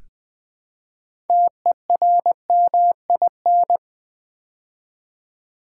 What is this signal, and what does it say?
Morse code, heard as a keyed tone, about 20 wpm, 700 Hz. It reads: TERMIN
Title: Build A Word - 2 Letters At A Time - 8-Letter Words - 20wpm